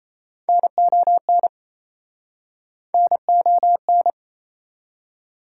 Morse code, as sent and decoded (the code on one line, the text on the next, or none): -. --- -. / -. --- -.
NON NON